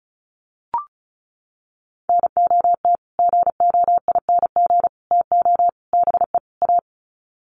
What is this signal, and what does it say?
Morse code, heard as a keyed tone, about 34 wpm, 700 Hz. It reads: NOT GOING TO BE A
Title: Sets of 5 Words Encore 35wpm